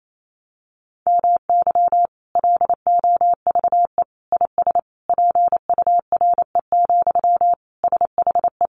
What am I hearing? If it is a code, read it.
Morse code, heard as a keyed tone, about 28 wpm, 700 Hz. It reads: MY LOVE IS PURE, SHE
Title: Sentences from Top 1500 Words 28wpm